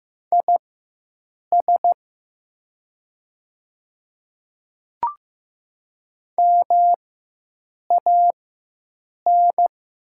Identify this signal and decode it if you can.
Morse code, heard as a keyed tone, about 15 wpm, spaced as for 7 wpm, 700 Hz. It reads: IS MAN
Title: 4x Character Spacing - Top 300 Words - 15wpm